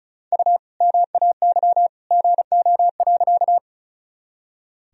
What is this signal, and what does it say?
Morse code, heard as a keyed tone, about 36 wpm, 700 Hz. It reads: U MAY GO.
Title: Sentences from Top 100 Words and Abbreviations 35wpm